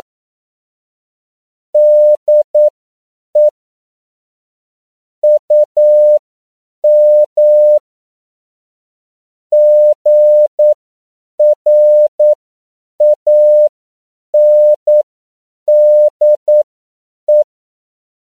Morse code, heard as a keyed tone, about 9 wpm, spaced as for 7 wpm, 600 Hz. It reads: DE UM GRANDE
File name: CW / 10PPM